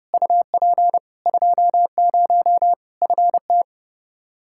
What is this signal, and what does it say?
Morse code, heard as a keyed tone, about 30 wpm, 700 Hz. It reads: UP 20 FT
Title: Practice QSO - Exchange 2 - Info - 30wpm